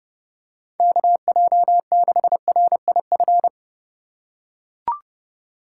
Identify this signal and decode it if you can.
Morse code, heard as a keyed tone, about 31 wpm, 700 Hz. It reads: KJ6RIF
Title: USA Call Signs - 6 Characters - 4x Character Spacing - 30wpm